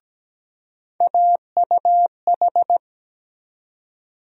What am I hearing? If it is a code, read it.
Morse code, heard as a keyed tone, about 17 wpm, 700 Hz. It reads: AUH
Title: USA Call Sign Postfixes - 3 Letters 17wpm